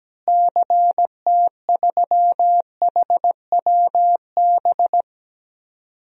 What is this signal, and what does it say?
Morse code, heard as a keyed tone, about 17 wpm, 700 Hz. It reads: CT3HWB
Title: DXCC-100 Call Signs - 6 Characters - Repeated 3x - 17wpm